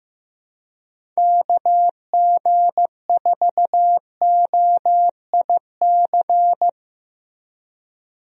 Morse code, heard as a keyed tone, about 15 wpm, 700 Hz. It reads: KG4OIC
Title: USA Call Signs - 6 Characters - Repeated 2x - 15wpm